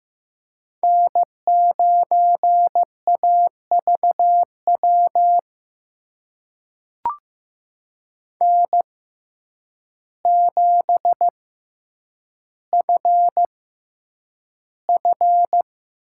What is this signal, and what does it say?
Morse code, heard as a keyed tone, about 15 wpm, spaced as for 8 wpm, 700 Hz. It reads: N9AVW N7FF
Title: USA Call Signs - 5 Characters - 6x Character Spacing - 15wpm